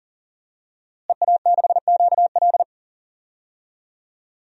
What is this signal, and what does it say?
Morse code, heard as a keyed tone, about 40 wpm, 700 Hz. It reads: EA6QL
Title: DXCC-100 Call Signs - 5 Characters - Repeated 2x - 40wpm